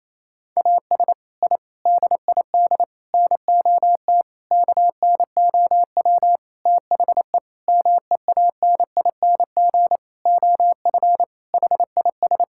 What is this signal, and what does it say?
Morse code, heard as a keyed tone, about 28 wpm, 700 Hz. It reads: AS I DID NOT KNOW THE MEANING OF HIS
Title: Sentences from Top 1000 Words - Review Part 6 - 28wpm